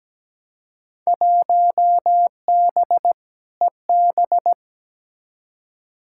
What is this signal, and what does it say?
Morse code, heard as a keyed tone, about 17 wpm, 700 Hz. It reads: 1B EB
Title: Field Day Exchanges Encore - 17wpm